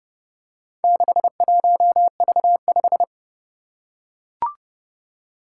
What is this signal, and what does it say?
Morse code, heard as a keyed tone, about 29 wpm, 700 Hz. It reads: 61V5
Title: Four Challenging Letter-Number Combinations 30wpm